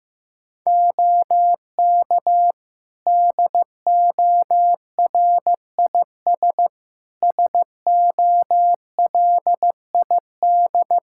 Morse code, heard as a keyed tone, about 15 wpm, 700 Hz. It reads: OK DORIS SOLID
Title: Practice QSO - Exchange 3 - More Info - 15wpm